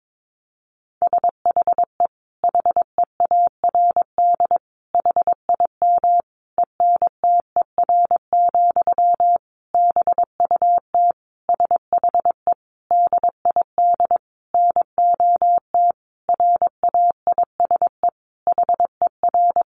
Morse code, heard as a keyed tone, about 22 wpm, 700 Hz. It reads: SHE HEARD HIM ENTER, BUT SHE DID NOT RAISE HER
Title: Sentences from Top 1400 Words 22wpm